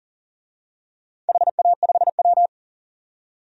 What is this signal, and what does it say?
Morse code, heard as a keyed tone, about 40 wpm, 700 Hz. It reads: HA5W
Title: DXCC-100 Call Signs - 4 Characters - 40wpm